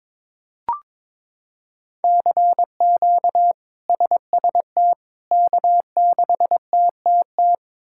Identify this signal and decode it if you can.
Morse code, heard as a keyed tone, about 22 wpm, 700 Hz. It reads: CQ SST K6TTT
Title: K1USN SST QSO Practice - 22wpm